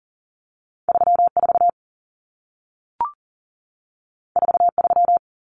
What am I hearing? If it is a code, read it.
Morse code, heard as a keyed tone, about 40 wpm, 700 Hz. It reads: 34 43
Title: Common Cut Numbers - 2-Digit 40wpm